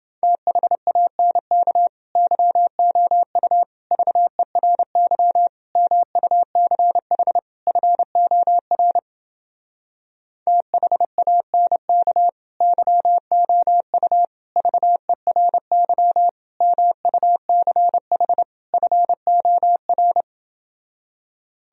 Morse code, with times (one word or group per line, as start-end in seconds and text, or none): 0.2-1.9 THANK
2.2-3.6 YOU
3.9-5.5 VERY
5.8-7.4 MUCH
7.7-9.0 FOR
10.5-12.3 THANK
12.6-14.2 YOU
14.6-16.3 VERY
16.6-18.4 MUCH
18.7-20.2 FOR